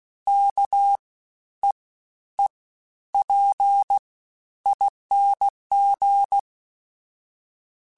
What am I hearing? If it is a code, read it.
Morse code, heard as a keyed tone, about 16 wpm, 800 Hz. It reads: K E E P ING
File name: Special INGs-mp3s / Special INGs-mp3s / SlowToFaster ing @15